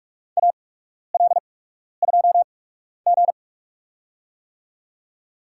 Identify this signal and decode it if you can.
Morse code, heard as a keyed tone, about 46 wpm, spaced as for 12 wpm, 690 Hz. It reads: AL2G
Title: USA Call Signs - 4 Characters - 8x Character Spacing - 45wpm